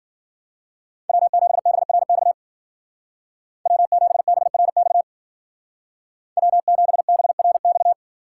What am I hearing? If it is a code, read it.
Morse code, heard as a keyed tone, about 50 wpm, 700 Hz. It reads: W7BRX W7BRX W7BRX
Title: Speed Racing - USA 1x3 Call Signs 40wpm